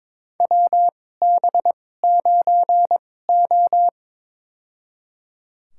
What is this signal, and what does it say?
Morse code, heard as a keyed tone, about 22 wpm, spaced as for 16 wpm, 700 Hz. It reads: WB9O
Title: USA Call Signs - 4 Characters - 2x Character Spacing - 22wpm